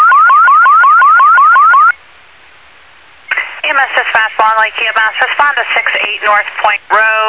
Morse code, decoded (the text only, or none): none